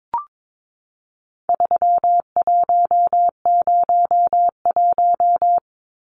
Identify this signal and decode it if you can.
Morse code, heard as a keyed tone, about 22 wpm, 700 Hz. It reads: 3101
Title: Numbers - 4-Digit 22wpm